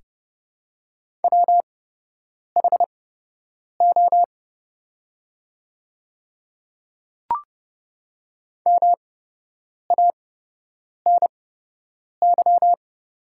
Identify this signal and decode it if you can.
Morse code, heard as a keyed tone, about 30 wpm, spaced as for 8 wpm, 690 Hz. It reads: WHO MANY
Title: 8x Character Spacing - Top 200 Words - 30wpm